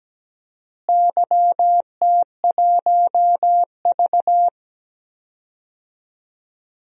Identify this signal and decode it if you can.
Morse code, heard as a keyed tone, about 17 wpm, 700 Hz. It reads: YT1V
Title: DXCC-100 Call Signs - 4 Characters - Repeated 2x - 17wpm